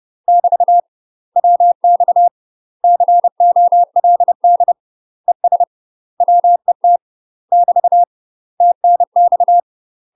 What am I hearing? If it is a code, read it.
Morse code, heard as a keyed tone, about 30 wpm, 700 Hz. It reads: <BT> WX COLD ES WET <BT> TNX
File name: UK MASTER QSOs TRAINING / 30WPM / UK MASTER QSOs 0045